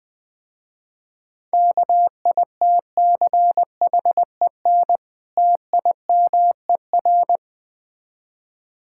Morse code, heard as a keyed tone, about 20 wpm, 700 Hz. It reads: KITCHEN TIMER
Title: Difficult Words - Speed-Racing with Hints - 20wpm